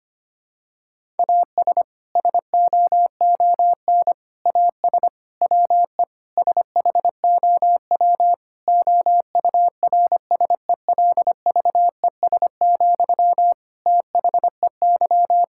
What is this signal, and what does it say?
Morse code, heard as a keyed tone, about 25 wpm, 700 Hz. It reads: AS SOON AS WE SHOW OURSELVES, THEY
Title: Sentences from Top 1600 Words 25wpm